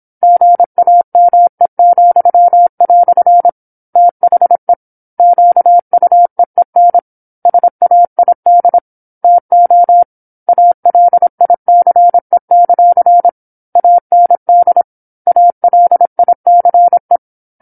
Morse code, heard as a keyed tone, about 26 wpm, 700 Hz. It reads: GAME," THE QUEEN SAID TO ALICE; AND ALICE
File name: 08-Chapter-08 26 WPM